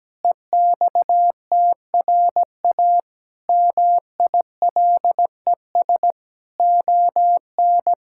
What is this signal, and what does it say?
Morse code, heard as a keyed tone, about 17 wpm, 700 Hz. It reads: EXTRA MILES ON
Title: Pangram Challenge v2 - 17wpm